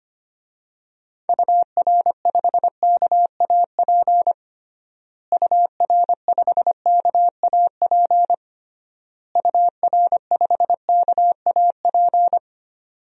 Morse code, heard as a keyed tone, about 25 wpm, 700 Hz. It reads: UR5KAP UR5KAP UR5KAP
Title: DXCC-100 Call Signs - 6 Characters - Repeated 3x - 25wpm